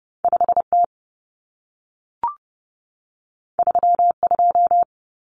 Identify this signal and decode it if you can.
Morse code, heard as a keyed tone, about 30 wpm, 700 Hz. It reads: 5T 32
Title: Common Cut Numbers - 2-Digit 30wpm